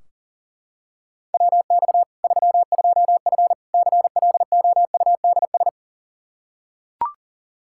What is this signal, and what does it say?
Morse code, heard as a keyed tone, about 37 wpm, 700 Hz. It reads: WX 32F CLOUDS
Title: Practice QSO - Exchange 2 - Info - 40wpm